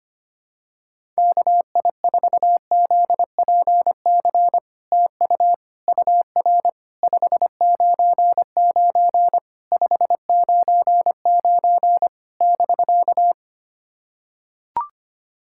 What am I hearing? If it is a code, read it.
Morse code, heard as a keyed tone, about 25 wpm, 700 Hz. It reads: KI4ZPC TU UR 599 599 <BK>
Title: SOTA QSO Practice 25wpm